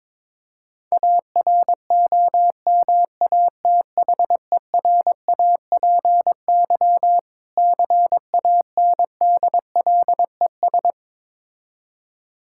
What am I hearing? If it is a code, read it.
Morse code, heard as a keyed tone, about 22 wpm, 690 Hz. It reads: AROMATHERAPY CANDLES
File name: Difficult Words with Hints - 22wpm